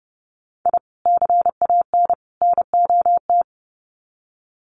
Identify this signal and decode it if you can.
Morse code, heard as a keyed tone, about 29 wpm, 700 Hz. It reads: I CAN NOT
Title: Sets of 3 Words 30wpm